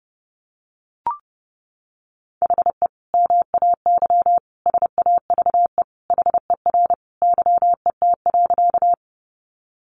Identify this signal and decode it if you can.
Morse code, heard as a keyed tone, about 30 wpm, 700 Hz. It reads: HE MAY SAVE HER YET.
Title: Sentences from Top 1100 Words 30wpm